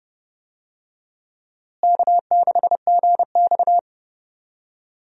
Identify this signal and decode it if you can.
Morse code, heard as a keyed tone, about 30 wpm, 700 Hz. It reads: K6GX
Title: USA Call Signs - 4 Characters - Repeated 2x - 30wpm